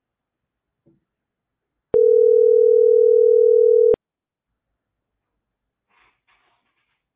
Ringback tone (North America), 440 + 480 Hz.